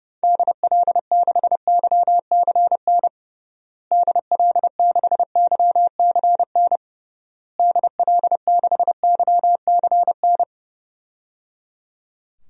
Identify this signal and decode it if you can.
Morse code, heard as a keyed tone, about 30 wpm, 700 Hz. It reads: DL6YCN DL6YCN DL6YCN
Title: DXCC-100 Call Signs - 6 Characters - Repeated 3x - 30wpm